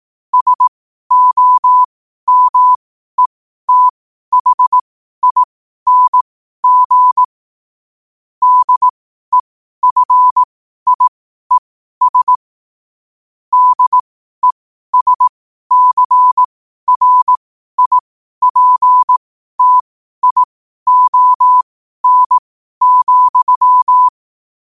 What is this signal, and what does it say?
Morse code, heard as a keyed tone, about 18 wpm, spaced as for 12 wpm, 1000 Hz. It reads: SOMETHING DEFIES DESCRIPTION,
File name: qotdfiles / 12wpm